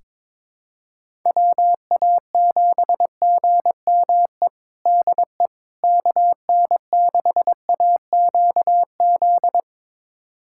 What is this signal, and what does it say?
Morse code, heard as a keyed tone, about 22 wpm, 700 Hz. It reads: WA7GME DE KN6AQZ
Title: USA Callsign DE Callsign 22wpm